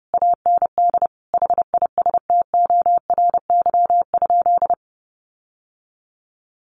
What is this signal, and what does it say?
Morse code, heard as a keyed tone, about 30 wpm, 700 Hz. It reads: AND HISTORY?
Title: Sentences from Top 600 Words 30wpm